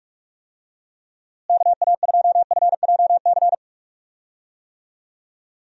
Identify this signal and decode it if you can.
Morse code, heard as a keyed tone, about 44 wpm, 700 Hz. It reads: KA2FJC
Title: USA Call Signs - 6 Characters - Repeated 2x - 45wpm